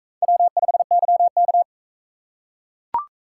Morse code, heard as a keyed tone, about 41 wpm, 700 Hz. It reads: W5YK